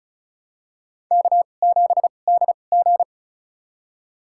Morse code, heard as a keyed tone, about 36 wpm, spaced as for 25 wpm, 700 Hz. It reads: K7DG
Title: USA Call Signs - 4 Characters - 2x Character Spacing - 35wpm